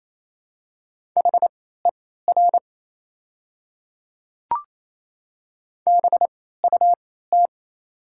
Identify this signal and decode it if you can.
Morse code, heard as a keyed tone, about 28 wpm, spaced as for 16 wpm, 700 Hz. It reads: HER BUT
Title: 3x Character Spacing - Top 100 Words - 28wpm